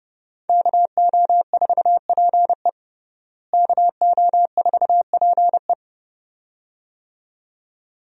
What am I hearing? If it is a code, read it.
Morse code, heard as a keyed tone, about 30 wpm, 700 Hz. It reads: KO4PE KO4PE
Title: USA Call Signs - 5 Characters - Repeated 2x - 30wpm